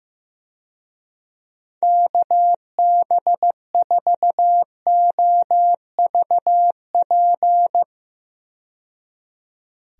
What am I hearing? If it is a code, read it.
Morse code, heard as a keyed tone, about 15 wpm, 700 Hz. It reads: KB4OVP
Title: USA Call Signs - 6 Characters - 15wpm